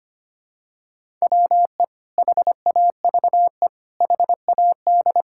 Morse code, heard as a keyed tone, about 25 wpm, 700 Hz. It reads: WE HAVE HAD
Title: Sentences from Top 1000 Words - Review Part 5 - 25wpm